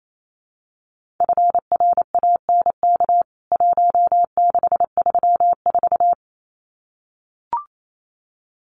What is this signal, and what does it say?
Morse code, heard as a keyed tone, about 28 wpm, 700 Hz. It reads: FRANK 1634
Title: CWT Exchange 28wpm